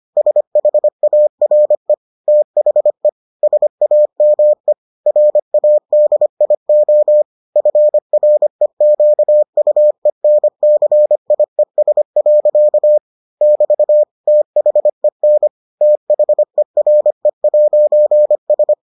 Morse code, heard as a keyed tone, about 25 wpm, 600 Hz. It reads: SHARE THE SAME RADIO FREQUENCIES. <BT> THEN THERE'S